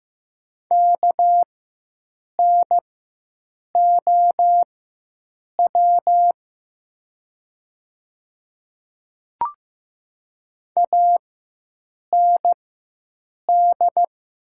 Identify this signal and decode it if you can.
Morse code, heard as a keyed tone, about 15 wpm, spaced as for 7 wpm, 700 Hz. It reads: KNOW AND
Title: 4x Character Spacing - Top 200 Words - 15wpm